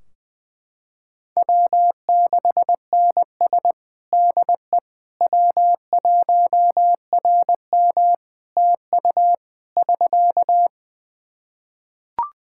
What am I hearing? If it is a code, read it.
Morse code, heard as a keyed tone, about 21 wpm, 700 Hz. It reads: W6NS DE W1RM TU <SK>
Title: K1USN SST QSO Practice 20wpm